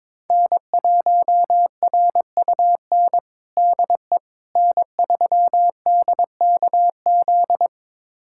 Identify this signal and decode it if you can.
Morse code, heard as a keyed tone, about 22 wpm, 700 Hz. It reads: N1RUN DE N3DKZ
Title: USA Callsign DE Callsign - 5 Characters - 22wpm